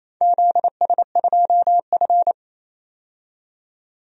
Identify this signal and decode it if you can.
Morse code, heard as a keyed tone, about 28 wpm, 700 Hz. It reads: ZS2F
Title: DXCC-100 Call Signs - 4 Characters - 28wpm